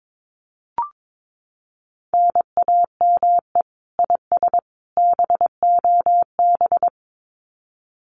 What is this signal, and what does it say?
Morse code, heard as a keyed tone, about 22 wpm, 700 Hz. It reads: NAME IS BOB
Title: Common QSO Phrases 22wpm